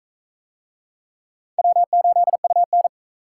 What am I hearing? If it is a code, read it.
Morse code, heard as a keyed tone, about 41 wpm, 700 Hz. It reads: W8UN